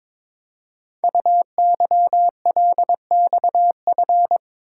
Morse code, heard as a keyed tone, about 22 wpm, 700 Hz. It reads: UYLXF
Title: Code Groups - 5 Letters - 22wpm